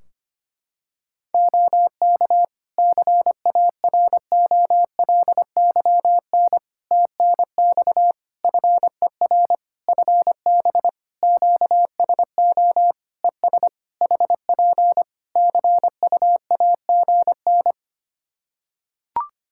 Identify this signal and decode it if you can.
Morse code, heard as a keyed tone, about 25 wpm, 700 Hz. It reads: OK CAROLYN TNX FER FB QSO ES HP CUAGN
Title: Practice QSO - Exchange 4 - Ending - 25wpm